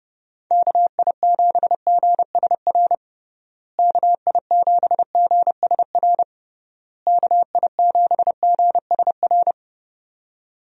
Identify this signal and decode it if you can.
Morse code, heard as a keyed tone, about 32 wpm, 700 Hz. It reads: KI7GSR KI7GSR KI7GSR
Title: USA Call Signs - 6 Characters - Repeated 3x - 30wpm